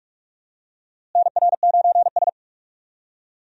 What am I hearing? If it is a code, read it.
Morse code, heard as a keyed tone, about 44 wpm, 700 Hz. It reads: NR9S
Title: USA Call Signs - 4 Characters - Repeated 2x - 45wpm